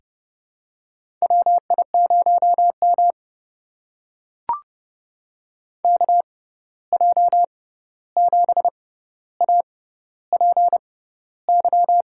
Morse code, heard as a keyed tone, about 30 wpm, spaced as for 14 wpm, 700 Hz. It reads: WI0M KJ7APY
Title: USA Call Signs - 6x Character Spacing - 30wpm